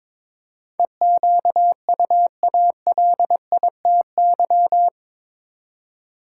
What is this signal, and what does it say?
Morse code, heard as a keyed tone, about 22 wpm, 700 Hz. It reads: EQUALITY